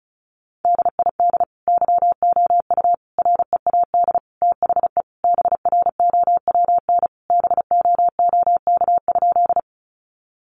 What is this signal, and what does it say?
Morse code, heard as a keyed tone, about 34 wpm, 700 Hz. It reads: DID YOU READ THE BROWN BOOK?
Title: Sentences from Top 1200 Words 35wpm